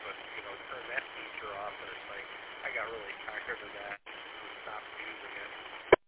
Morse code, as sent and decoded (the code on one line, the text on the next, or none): none